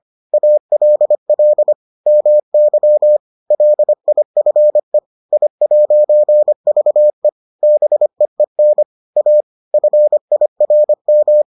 Morse code, heard as a keyed tone, about 25 wpm, 600 Hz. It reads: ALL MY LIFE I'VE BEEN A FIRM